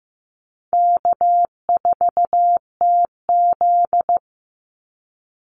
Morse code, heard as a keyed tone, about 15 wpm, 700 Hz. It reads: K4TZ
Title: Easy Call Signs Repeated 3x - 15wpm